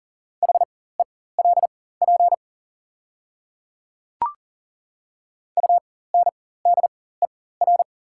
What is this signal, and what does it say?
Morse code, heard as a keyed tone, about 40 wpm, spaced as for 19 wpm, 690 Hz. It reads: HELP UNDER
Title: 4x Character Spacing - Top 300 Words - 40wpm